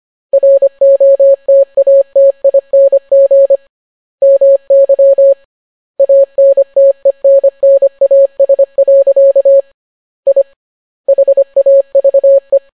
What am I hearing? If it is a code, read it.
Morse code, heard as a keyed tone, about 25 wpm, 550 Hz. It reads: ROTATING MY ANTENNAS. I HAVE